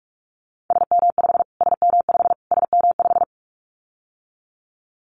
Morse code, heard as a keyed tone, about 44 wpm, 700 Hz. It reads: SM5 SM5 SM5
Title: Callsign Prefixes for Top 25 DXCC - Repeated 3x - 45wpm